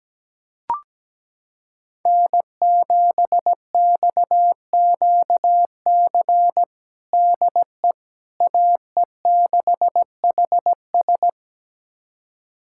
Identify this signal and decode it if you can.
Morse code, heard as a keyed tone, about 17 wpm, 700 Hz. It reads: N7XQC DE AE6HS
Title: USA Callsign DE Callsign - 5 Characters - 17wpm